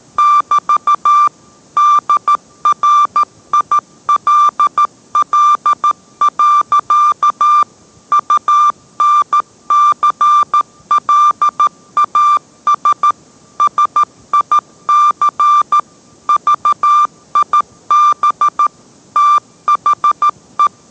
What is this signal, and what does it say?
Morse code, heard as a keyed tone, about 13 wpm, 1180 Hz. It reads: <BT> DRILL. UNCLAS SIC VIB THE